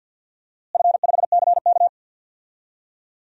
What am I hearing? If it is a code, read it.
Morse code, heard as a keyed tone, about 51 wpm, 700 Hz. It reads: F5CK
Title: DXCC-100 Call Signs - 4 Characters - Repeated 3x - 50wpm